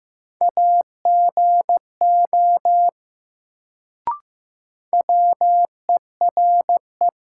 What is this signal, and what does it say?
Morse code, heard as a keyed tone, about 15 wpm, 700 Hz. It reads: AGO WERE
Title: Sets of 2 Words - New Word Drill - 15wpm